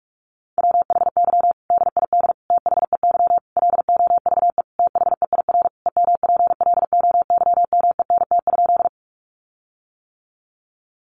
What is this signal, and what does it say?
Morse code, heard as a keyed tone, about 44 wpm, 700 Hz. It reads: WHY DID THEY LOVE THEIR EMPLOYMENT?
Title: Sentences from Top 1500 Words 45wpm